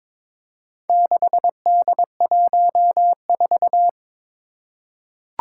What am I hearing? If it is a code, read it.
Morse code, heard as a keyed tone, about 22 wpm, 700 Hz. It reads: 6D14
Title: Four Challenging Letter-Number Combinations 22wpm